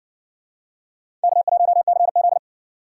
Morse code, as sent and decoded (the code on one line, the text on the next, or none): -.- .---- -.-. --..
K1CZ